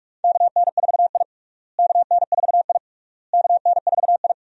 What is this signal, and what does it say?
Morse code, heard as a keyed tone, about 42 wpm, 700 Hz. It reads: KN4I KN4I KN4I
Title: USA Call Signs - 4 Characters - Repeated 3x - 45wpm